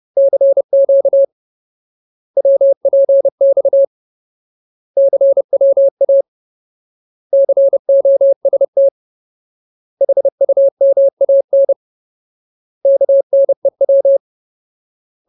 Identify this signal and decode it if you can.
Morse code, heard as a keyed tone, about 30 wpm, 550 Hz. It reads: CQ WPX CWA COST HUMAN KNEW
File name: Ultimate Word List / 30WPM Ultimate